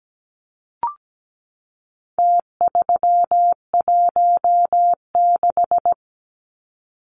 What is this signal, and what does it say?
Morse code, heard as a keyed tone, about 17 wpm, 700 Hz. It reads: T316
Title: Numbers - Contest Style - 4-Digit 17wpm